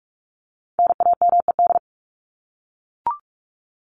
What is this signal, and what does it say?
Morse code, heard as a keyed tone, about 44 wpm, 700 Hz. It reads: NAMED
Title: ED Suffix - Top 100 Words with Base Words 45wpm